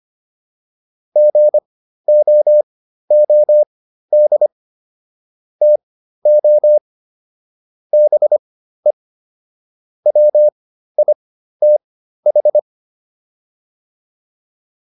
Morse code, heard as a keyed tone, about 25 wpm, spaced as for 13 wpm, 600 Hz. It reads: GOOD TO BE WITH